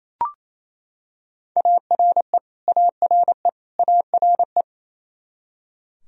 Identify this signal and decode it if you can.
Morse code, heard as a keyed tone, about 28 wpm, 700 Hz. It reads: ARE ARE ARE
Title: Top 100 Words Repeated 28wpm